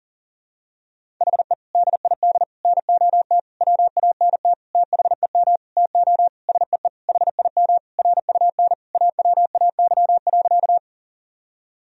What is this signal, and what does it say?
Morse code, heard as a keyed tone, about 40 wpm, 700 Hz. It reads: HE DID NOT WANT THEM TO SEE HIM RUN AWAY.
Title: Sentences from Top 1000 Words - Review Part 1 - 40wpm